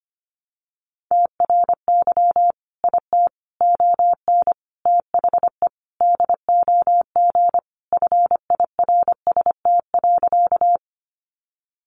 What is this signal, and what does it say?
Morse code, heard as a keyed tone, about 25 wpm, 700 Hz. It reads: TRY IT ON THE DOG FIRST.
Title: Sentences from Top 400 Words 25wpm